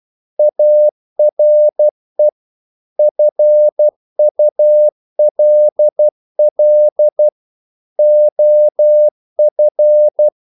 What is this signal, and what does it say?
Morse code, heard as a keyed tone, about 12 wpm, 600 Hz. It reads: ARE FULL OF
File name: CW / Practice / EnglishQuotations0312